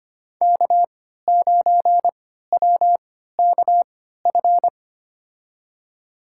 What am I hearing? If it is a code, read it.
Morse code, heard as a keyed tone, about 25 wpm, spaced as for 14 wpm, 700 Hz. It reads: K9WKF
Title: USA Call Signs - 5 Characters - 3x Character Spacing - 25wpm